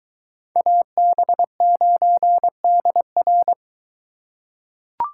Morse code, heard as a keyed tone, about 21 wpm, 700 Hz. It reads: AB9DR